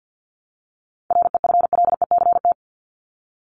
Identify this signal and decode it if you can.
Morse code, heard as a keyed tone, about 50 wpm, 700 Hz. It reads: REFLECT
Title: Top 1251-1500 Words 50wpm